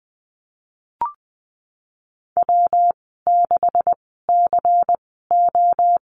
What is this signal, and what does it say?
Morse code, heard as a keyed tone, about 20 wpm, spaced as for 15 wpm, 700 Hz. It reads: W6CO
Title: 128 Lesson 21g - Number 6 - Call Signs 20wpm